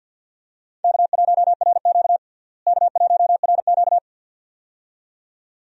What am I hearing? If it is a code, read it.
Morse code, heard as a keyed tone, about 52 wpm, 700 Hz. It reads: K1RQ K1RQ
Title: USA Call Signs - 4 Characters - Repeated 2x - 50wpm